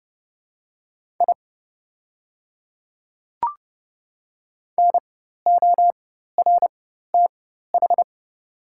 Morse code, heard as a keyed tone, about 30 wpm, spaced as for 14 wpm, 700 Hz. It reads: I NORTH
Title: 4x Character Spacing - Top 300 Words - 30wpm